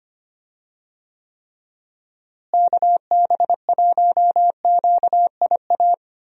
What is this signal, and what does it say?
Morse code, heard as a keyed tone, about 25 wpm, 700 Hz. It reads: KB1QIA
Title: USA Call Signs - 6 Characters - Repeated 3x - 25wpm